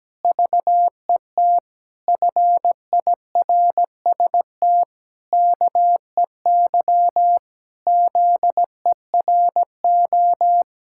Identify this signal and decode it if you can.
Morse code, heard as a keyed tone, about 17 wpm, 700 Hz. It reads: VET FIRST KEY ZERO
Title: Pangram Challenge v2 - 17wpm